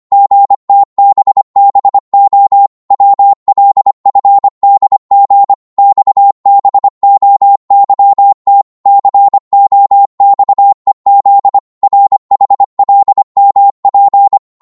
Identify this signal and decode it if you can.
Morse code, heard as a keyed tone, about 24 wpm, 800 Hz. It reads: GTBBO WLFDG XBOYT COXEZ RHLMP